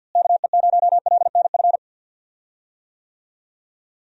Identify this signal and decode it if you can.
Morse code, heard as a keyed tone, about 50 wpm, 700 Hz. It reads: KE0LNF
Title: USA Call Signs - 6 Characters - Repeated 3x - 50wpm